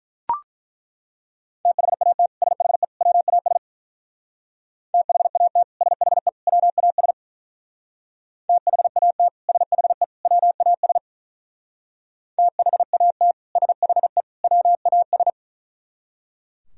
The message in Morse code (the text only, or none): THAT SHE WAS THAT SHE WAS THAT SHE WAS THAT SHE WAS